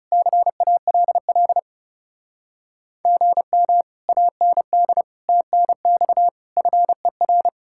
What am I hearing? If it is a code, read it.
Morse code, heard as a keyed tone, about 36 wpm, 700 Hz. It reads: CALL GM AND TNX FER
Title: Speed-Racing Common QSO Phrases - 30wpm